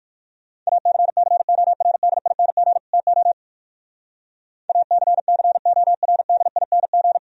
Morse code, heard as a keyed tone, about 54 wpm, 700 Hz. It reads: ACCORDING TO ACCORDING
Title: Speed-Racing - Sets of 2 Words 45wpm